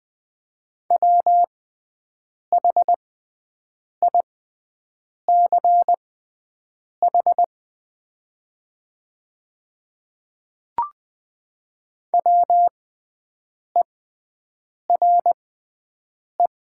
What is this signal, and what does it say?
Morse code, heard as a keyed tone, about 20 wpm, spaced as for 7 wpm, 700 Hz. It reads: WHICH WERE
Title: 6x Character Spacing - Top 100 Words - 20wpm